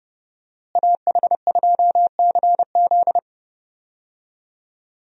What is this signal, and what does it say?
Morse code, heard as a keyed tone, about 30 wpm, 700 Hz. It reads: AH2CZ
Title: USA Call Signs - 5 Characters - Repeated 3x - 30wpm